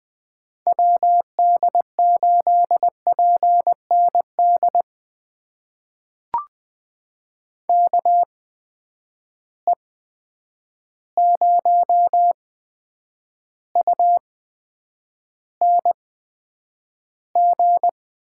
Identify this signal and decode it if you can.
Morse code, heard as a keyed tone, about 20 wpm, spaced as for 9 wpm, 700 Hz. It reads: WD8PND KE0UNG
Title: USA Call Signs - 6 Characters - 8x Character Spacing - 20wpm